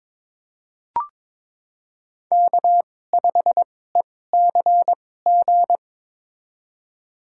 Morse code, heard as a keyed tone, about 22 wpm, spaced as for 16 wpm, 700 Hz. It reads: K5ECG
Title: USA Call Signs - 5 Characters - 2x Character Spacing - 22wpm